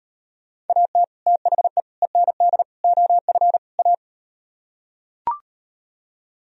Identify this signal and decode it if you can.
Morse code, heard as a keyed tone, about 38 wpm, 700 Hz. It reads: AT THE END OF A